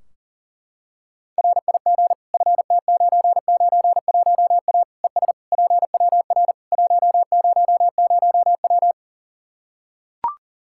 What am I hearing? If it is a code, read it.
Morse code, heard as a keyed tone, about 37 wpm, 700 Hz. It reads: RIG FT991A ES PWR 100W
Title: Practice QSO - Exchange 2 - Info - 40wpm